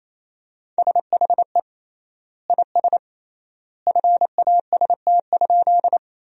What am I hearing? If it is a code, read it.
Morse code, heard as a keyed tone, about 28 wpm, 700 Hz. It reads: SHE IS FAST?
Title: 3x Word-Spacing - Sentences from Top 400 Words - 28wpm